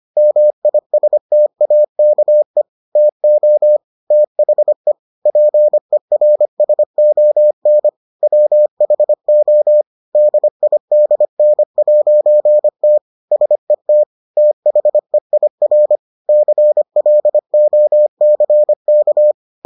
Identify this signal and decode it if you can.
Morse code, heard as a keyed tone, about 25 wpm, 600 Hz. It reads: MISTAKE TO THE PERSON WHO DIDN'T SET THEIR CLOCK